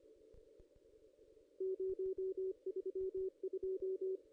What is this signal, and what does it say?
Morse code, heard as a keyed tone, about 25 wpm, 380 Hz. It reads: 032
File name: contests / ss-cw-k5tr-op-n5rz -1900